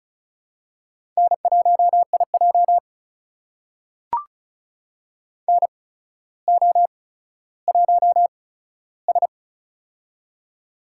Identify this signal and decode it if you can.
Morse code, heard as a keyed tone, about 36 wpm, spaced as for 15 wpm, 700 Hz. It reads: N1IJ NO1S
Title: USA Call Signs - 4 Characters - 8x Character Spacing - 35wpm